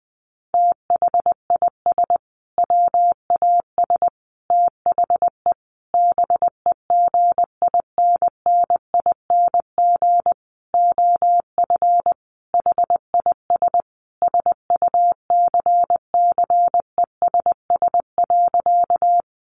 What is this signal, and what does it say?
Morse code, heard as a keyed tone, about 20 wpm, 700 Hz. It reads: THIS WAS THE BEGINNING OF HIS SUCCESS.
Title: Sentences from Top 900 Words 20wpm